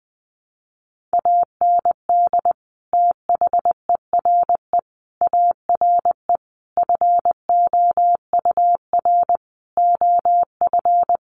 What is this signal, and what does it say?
Morse code, heard as a keyed tone, about 20 wpm, 700 Hz. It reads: AND THERE ARE FOUR OF
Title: Sentences from Top 300 Words 20wpm